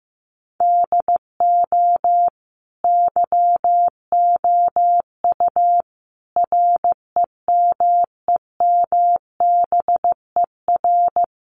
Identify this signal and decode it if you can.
Morse code, heard as a keyed tone, about 15 wpm, 700 Hz. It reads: DO YOU REMEMBER
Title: Sentences from Top 700 Words 15wpm